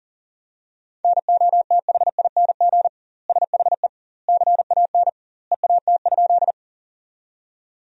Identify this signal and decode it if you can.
Morse code, heard as a keyed tone, about 37 wpm, 700 Hz. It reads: NOTHING SHE CAN EAT?
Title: 2x Word-Spacing - Sentences from Top 500 Words - 40wpm